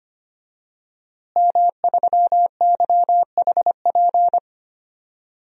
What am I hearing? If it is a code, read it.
Morse code, heard as a keyed tone, about 25 wpm, 700 Hz. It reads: M3YHP
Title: DXCC-100 Call Signs - 5 Characters - Repeated 2x - 25wpm